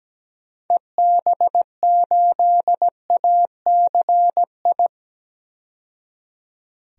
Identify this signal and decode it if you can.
Morse code, heard as a keyed tone, about 17 wpm, 700 Hz. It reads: EB8ACI
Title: DXCC-100 Call Signs - 6 Characters - 17wpm